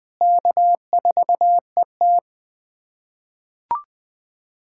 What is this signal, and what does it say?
Morse code, heard as a keyed tone, about 20 wpm, 700 Hz. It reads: K4ET
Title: USA Callsign DE Callsign - 4 Characters - 20wpm